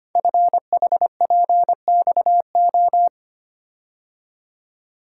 Morse code, heard as a keyed tone, about 25 wpm, 700 Hz. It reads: FHPXO
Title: Code Groups - 5 Letters - 25wpm